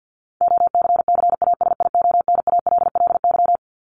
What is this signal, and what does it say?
Morse code, heard as a keyed tone, about 50 wpm, 700 Hz. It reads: OCCASIONALLY
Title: LY Suffix - Top 100 Words with Base Words 50wpm